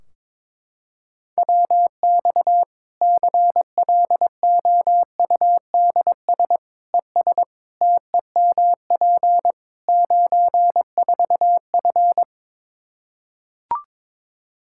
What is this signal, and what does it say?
Morse code, heard as a keyed tone, about 21 wpm, 700 Hz. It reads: WX CLOUDS ES TEMP 94F
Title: Practice QSO - Exchange 2 - Info - 22wpm